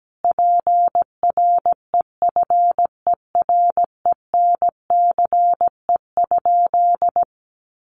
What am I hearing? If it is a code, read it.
Morse code, heard as a keyed tone, about 17 wpm, 700 Hz. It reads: PREFERENCE?
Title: Sentences from Top 1500 Words 17wpm